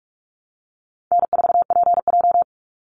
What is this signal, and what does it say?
Morse code, heard as a keyed tone, about 44 wpm, 700 Hz. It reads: N4PJ
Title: Easy Call Signs Repeated 2x - 45wpm